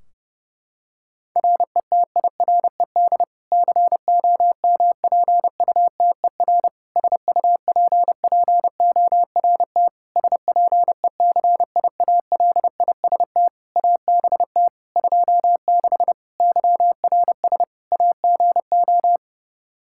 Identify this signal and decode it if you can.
Morse code, heard as a keyed tone, about 30 wpm, 700 Hz. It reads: RETIRED COMPUTER SUPPORT SPECIALIST ABT 26 YRS AGO
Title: Practice QSO - Exchange 3 - More Info - 30wpm